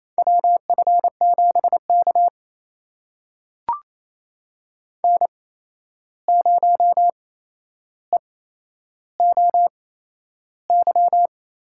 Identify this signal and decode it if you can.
Morse code, heard as a keyed tone, about 28 wpm, spaced as for 11 wpm, 700 Hz. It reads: WF7K N0EOY
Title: USA Call Signs - 8x Character Spacing - 28wpm